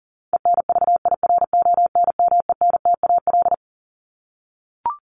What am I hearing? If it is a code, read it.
Morse code, heard as a keyed tone, about 40 wpm, 700 Hz. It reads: ENVIRONMENTAL
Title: Top 1001-1250 Words 40wpm